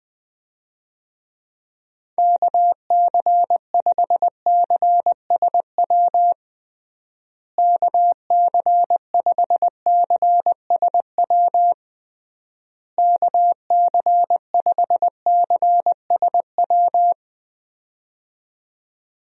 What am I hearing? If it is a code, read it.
Morse code, heard as a keyed tone, about 20 wpm, 700 Hz. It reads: KC5CSW KC5CSW KC5CSW
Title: USA Call Signs - 6 Characters - Repeated 3x - 20wpm